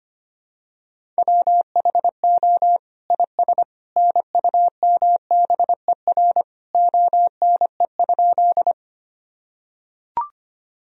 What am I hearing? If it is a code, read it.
Morse code, heard as a keyed tone, about 25 wpm, 700 Hz. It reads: WHO IS NUMBER ONE?
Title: Sentences from Top 1000 Words - Review Part 1 - 25wpm